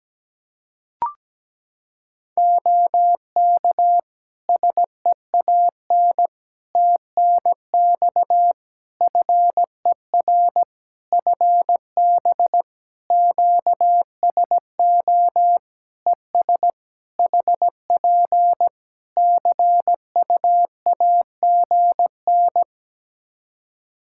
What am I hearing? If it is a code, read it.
Morse code, heard as a keyed tone, about 17 wpm, 700 Hz. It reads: OK SEAN TNX FER FB QSO ES HP CUAGN
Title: Practice QSO - Exchange 4 - Ending - 17wpm